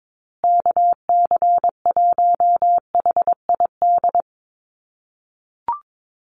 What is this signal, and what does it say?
Morse code, heard as a keyed tone, about 22 wpm, 700 Hz. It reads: KC1HID
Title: USA Call Signs Repeated 3x 22wpm